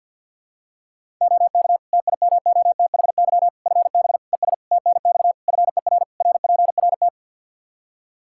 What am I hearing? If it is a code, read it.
Morse code, heard as a keyed tone, about 50 wpm, 700 Hz. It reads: OK TIMOTHY FB ES TNX FER RPRT
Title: Practice QSO - Exchange 2 - Info - 50wpm